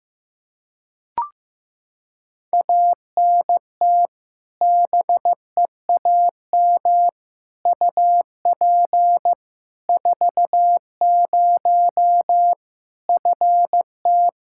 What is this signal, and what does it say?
Morse code, heard as a keyed tone, about 15 wpm, 700 Hz. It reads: ANT BEAM UP 40 FT
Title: Practice QSO - Exchange 2 - Info - 15wpm